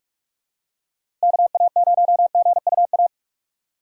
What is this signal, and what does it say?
Morse code, heard as a keyed tone, about 46 wpm, 700 Hz. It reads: KA0GUA
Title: USA Call Signs - 6 Characters - 6x Character Spacing - 45wpm